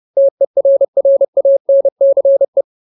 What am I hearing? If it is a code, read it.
Morse code, heard as a keyed tone, about 30 wpm, 550 Hz. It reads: TERRANCE
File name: Ultimate Word List / 30WPM Ultimate